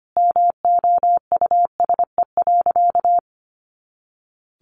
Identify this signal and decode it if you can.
Morse code, heard as a keyed tone, about 25 wpm, 700 Hz. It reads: MOUSE.
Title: A Sound Of Thunder 25wpm